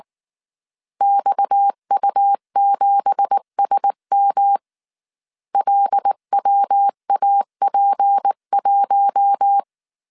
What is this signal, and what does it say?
Morse code, heard as a keyed tone, about 19 wpm, 800 Hz. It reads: XU7SM LWAP1
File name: CW / DET / 1 9 1500C S C 18wpm 2539